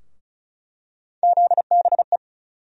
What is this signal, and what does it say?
Morse code, heard as a keyed tone, about 34 wpm, 700 Hz. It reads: ZBE
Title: USA Call Sign Postfixes - 3 Letters 35wpm